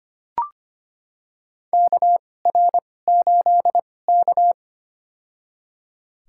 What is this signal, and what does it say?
Morse code, heard as a keyed tone, about 25 wpm, spaced as for 18 wpm, 700 Hz. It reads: KR8K
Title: USA Call Signs - 4 Characters - 2x Character Spacing - 25wpm